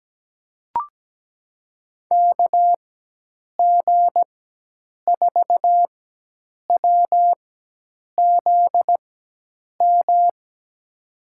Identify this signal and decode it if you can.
Morse code, heard as a keyed tone, about 17 wpm, spaced as for 8 wpm, 700 Hz. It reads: KG4WZM
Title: USA Call Signs - 4x Character Spacing - 17wpm